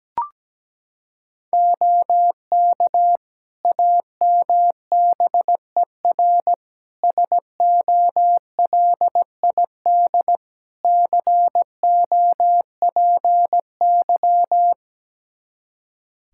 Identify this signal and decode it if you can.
Morse code, heard as a keyed tone, about 17 wpm, 700 Hz. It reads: OK AMBER SOLID COPY
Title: Practice QSO - Exchange 3 - More Info - 17wpm